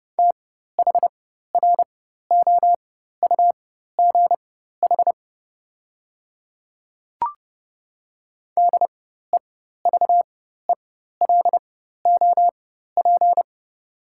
Morse code, heard as a keyed tone, about 30 wpm, spaced as for 14 wpm, 700 Hz. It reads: THROUGH DEVELOP
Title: 4x Character Spacing - Top 500 Words - 30wpm